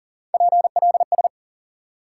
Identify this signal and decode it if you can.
Morse code, heard as a keyed tone, about 40 wpm, 700 Hz. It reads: PLS
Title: Top 90 QSO Elements 40wpm